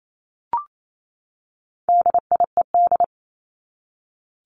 Morse code, heard as a keyed tone, about 28 wpm, 700 Hz. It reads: DIED
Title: ED Suffix - Top 100 Words with Base Words 28wpm